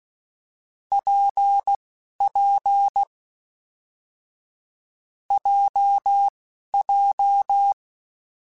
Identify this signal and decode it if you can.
Morse code, heard as a keyed tone, about 16 wpm, spaced as for 11 wpm, 800 Hz. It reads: PP JJ
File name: Difficult Combos / Difficult Combos-mp3s / WPJ WPJ1 @10